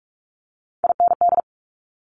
Morse code, heard as a keyed tone, about 46 wpm, 700 Hz. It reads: IND
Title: Multi-letter Phonemes Quick Fire 45wpm